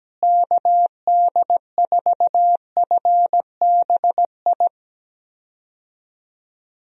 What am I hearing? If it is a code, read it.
Morse code, heard as a keyed tone, about 17 wpm, 700 Hz. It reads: KD4FBI
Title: USA Call Signs - 6 Characters - Repeated 3x - 17wpm